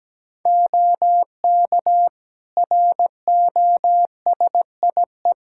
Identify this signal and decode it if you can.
Morse code, heard as a keyed tone, about 17 wpm, 700 Hz. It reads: OK ROSIE
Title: Practice QSO - Exchange 3 - More Info - 17wpm